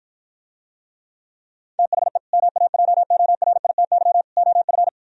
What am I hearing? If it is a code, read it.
Morse code, heard as a keyed tone, about 54 wpm, 700 Hz. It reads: THE MAJORITY OF